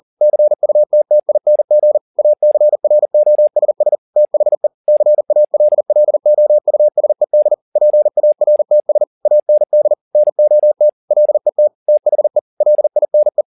Morse code, heard as a keyed tone, about 40 wpm, 600 Hz. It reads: CUTTING ACROSS THE CALLOUSED PARTS AND NOT LET THE LINE
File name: the old man and the sea 2 40wpm